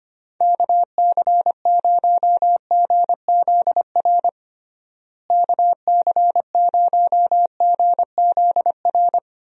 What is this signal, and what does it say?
Morse code, heard as a keyed tone, about 25 wpm, 700 Hz. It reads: KC0GZR KC0GZR
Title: USA Call Signs - 6 Characters - Repeated 2x - 25wpm